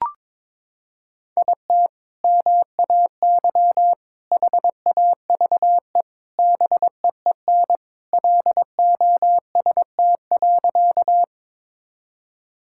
Morse code, heard as a keyed tone, about 22 wpm, 700 Hz. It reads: IT MAY HAVE BEEN LOST.
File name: practice / 22wpm / Sentences with 3-Word Phrases - 22wpm